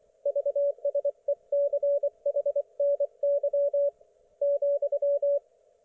Morse code, heard as a keyed tone, about 24 wpm, 570 Hz. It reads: VSECHNY ,
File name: Archiv / CUC224 / ok1fhi cul224